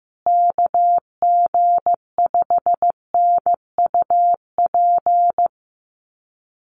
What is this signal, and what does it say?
Morse code, heard as a keyed tone, about 15 wpm, 700 Hz. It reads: KG5NUP
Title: USA Call Signs Repeated 2x 15wpm